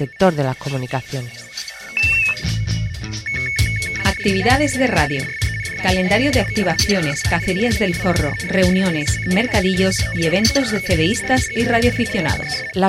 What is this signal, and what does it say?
Two-tone teleprinter signal, about 420 Hz shift, with speech.